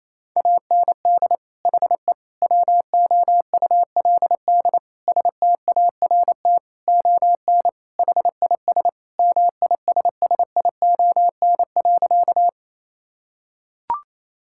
Morse code, heard as a keyed tone, about 28 wpm, 700 Hz. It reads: AND HE WOULD START ON HIS MISSION.
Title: Sentences from Top 1600 Words 28wpm